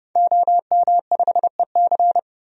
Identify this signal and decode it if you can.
Morse code, heard as a keyed tone, about 30 wpm, 700 Hz. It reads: OM5EC
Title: DXCC-100 Call Signs - 5 Characters - 30wpm